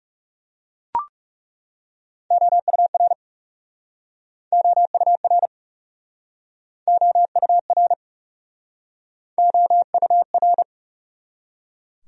Morse code, heard as a keyed tone, about 40 wpm, 700 Hz. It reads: OUR OUR OUR OUR